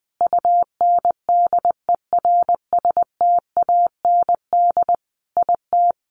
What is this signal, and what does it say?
Morse code, heard as a keyed tone, about 19 wpm, 700 Hz. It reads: UNDERSTAND IT
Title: Sentences from Top 1300 Words 20wpm